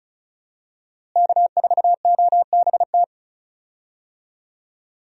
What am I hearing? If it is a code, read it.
Morse code, heard as a keyed tone, about 36 wpm, 700 Hz. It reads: K4OBT
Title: USA Call Signs - 5 Characters - Repeated 3x - 35wpm